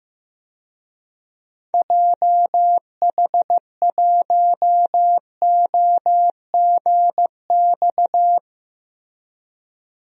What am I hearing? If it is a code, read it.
Morse code, heard as a keyed tone, about 15 wpm, 700 Hz. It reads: JH1OGX
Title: DXCC-100 Call Signs - 6 Characters - Repeated 3x - 15wpm